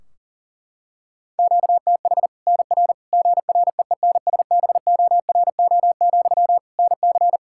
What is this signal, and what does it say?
Morse code, heard as a keyed tone, about 37 wpm, 700 Hz. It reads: QTH NR GREENSBORO, NC
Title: Practice QSO - Exchange 1 - Beginning - 40wpm